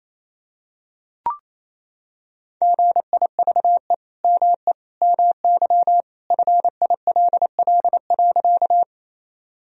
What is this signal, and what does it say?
Morse code, heard as a keyed tone, about 28 wpm, 700 Hz. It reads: GIVE ME MY FILL.
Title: Sentences from Top 1000 Words - Review Part 3 - 28wpm